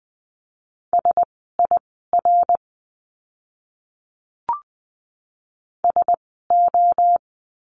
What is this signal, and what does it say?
Morse code, heard as a keyed tone, about 20 wpm, spaced as for 15 wpm, 700 Hz. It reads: SIR SO
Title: 061 Lesson 11f - Letter h - Words 20wpm